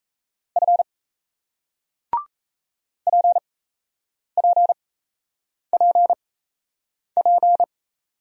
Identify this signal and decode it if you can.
Morse code, heard as a keyed tone, about 38 wpm, spaced as for 8 wpm, 700 Hz. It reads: F PPPP